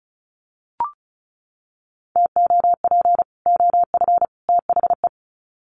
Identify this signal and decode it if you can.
Morse code, heard as a keyed tone, about 34 wpm, 700 Hz. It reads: TOP OF THE
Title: Sets of 3 Words 35wpm